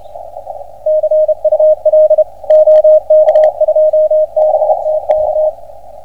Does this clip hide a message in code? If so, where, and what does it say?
Yes: Morse code from 0.9 to 5.5 s, heard as a keyed tone, about 29 wpm, 620 Hz, with radio noise. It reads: CUL OK2BTK